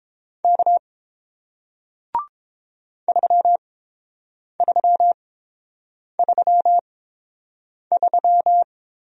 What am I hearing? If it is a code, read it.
Morse code, heard as a keyed tone, about 30 wpm, spaced as for 7 wpm, 700 Hz. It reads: K 3333